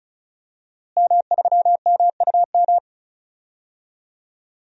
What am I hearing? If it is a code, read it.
Morse code, heard as a keyed tone, about 34 wpm, 700 Hz. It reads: M3MUM
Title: DXCC-100 Call Signs - 5 Characters - Repeated 2x - 35wpm